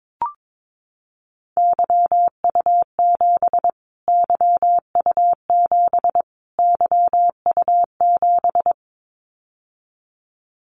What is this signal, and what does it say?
Morse code, heard as a keyed tone, about 22 wpm, 700 Hz. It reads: YU7 YU7 YU7
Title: Callsign Prefixes for Top 25 DXCC - Repeated 3x - 22wpm